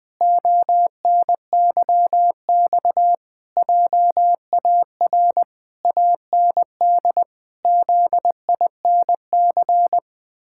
Morse code, heard as a keyed tone, about 21 wpm, 700 Hz. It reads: ONYX JAR AND ZINC
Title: Pangram Challenge v2 - 20wpm